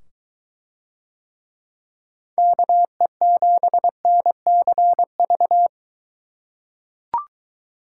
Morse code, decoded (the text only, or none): KE7NCV